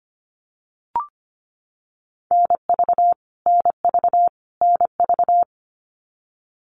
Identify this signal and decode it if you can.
Morse code, heard as a keyed tone, about 25 wpm, 700 Hz. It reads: NV NV NV
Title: US States Repeated 3x 25wpm